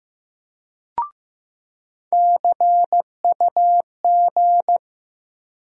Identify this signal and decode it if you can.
Morse code, heard as a keyed tone, about 15 wpm, 700 Hz. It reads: CUG
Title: USA Call Sign Postfixes - 3 Letters 15wpm